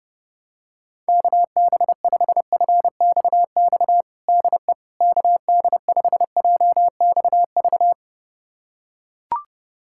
Morse code, heard as a keyed tone, about 32 wpm, 700 Hz. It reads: KB5FXX DE KD5JXV
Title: USA Callsign DE Callsign - 6 Characters - 30wpm